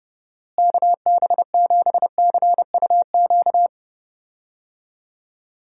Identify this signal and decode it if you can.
Morse code, heard as a keyed tone, about 32 wpm, 700 Hz. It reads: KB7CUQ
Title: USA Call Signs - 6 Characters - Repeated 3x - 30wpm